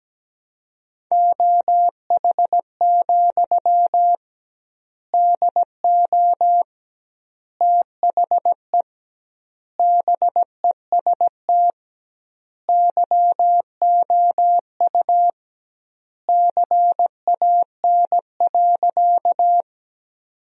Morse code, heard as a keyed tone, about 17 wpm, 700 Hz. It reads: OH, DO THE BEST YOU CAN.
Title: 2x Word-Spacing - Sentences from Top 500 Words - 17wpm